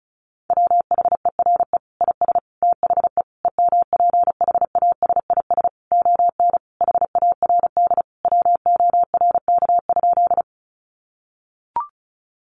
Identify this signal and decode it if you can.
Morse code, heard as a keyed tone, about 34 wpm, 700 Hz. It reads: WHERE IS THE EMPHASIS ON HARD WORK?
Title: Sentences from Top 1400 Words 35wpm